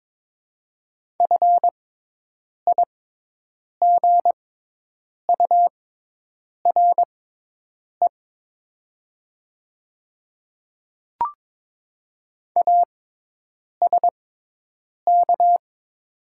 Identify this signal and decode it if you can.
Morse code, heard as a keyed tone, about 22 wpm, spaced as for 8 wpm, 700 Hz. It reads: FIGURE ASK
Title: 6x Character Spacing - Top 500 Words - 22wpm